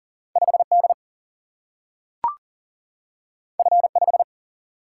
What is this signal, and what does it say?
Morse code, heard as a keyed tone, about 40 wpm, 700 Hz. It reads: 5D F5
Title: Single-Letter Single-Number 40wpm